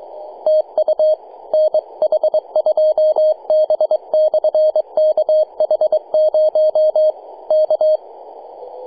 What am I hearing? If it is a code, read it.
Morse code, heard as a keyed tone, about 22 wpm, 620 Hz. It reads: TU NH2B/KH0 K
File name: posts / 2015-sept-13-0732z-nh2b-kh0-18-082-cw